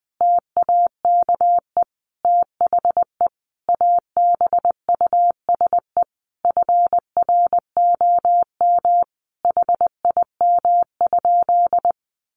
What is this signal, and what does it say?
Morse code, heard as a keyed tone, about 20 wpm, 700 Hz. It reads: TAKE THE ABUSE FROM HIM?
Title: Sentences from Top 1300 Words 20wpm